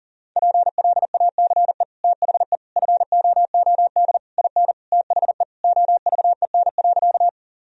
Morse code, heard as a keyed tone, about 37 wpm, 700 Hz. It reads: PLACE THE FOOD IN THE OVEN.
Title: Sentences from Top 1000 Words - Review Part 4 - 40wpm